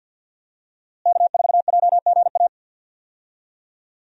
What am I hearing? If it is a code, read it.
Morse code, heard as a keyed tone, about 52 wpm, 700 Hz. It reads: K4JGA
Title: USA Call Signs - 5 Characters - Repeated 3x - 50wpm